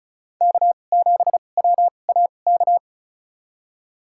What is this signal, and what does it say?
Morse code, heard as a keyed tone, about 34 wpm, spaced as for 25 wpm, 700 Hz. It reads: K7WAK
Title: USA Call Signs - 2x Character Spacing - 35wpm